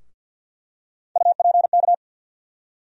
Morse code, heard as a keyed tone, about 48 wpm, 700 Hz. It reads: UPK
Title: USA Call Sign Postfixes - 3 Letters 50wpm